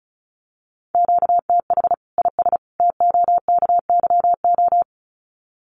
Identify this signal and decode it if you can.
Morse code, heard as a keyed tone, about 34 wpm, 700 Hz. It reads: QTH IS TOKYO
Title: Common QSO Phrases 35wpm